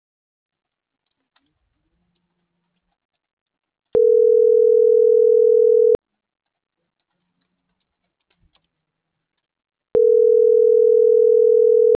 Ringback tone (North America), 440 + 480 Hz.
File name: CENTRAL MAERSK / out gabrielle.melendez - .5439